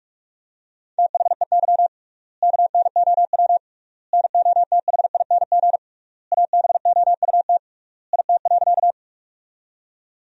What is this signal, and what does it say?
Morse code, heard as a keyed tone, about 41 wpm, 700 Hz. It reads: THEY KNOW NOTHING ABOUT IT.
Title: 3x Word-Spacing - Sentences from Top 500 Words - 45wpm